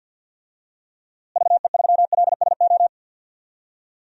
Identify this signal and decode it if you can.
Morse code, heard as a keyed tone, about 53 wpm, 700 Hz. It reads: VE3LSO
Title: DXCC-100 Call Signs - 6 Characters - Repeated 2x - 50wpm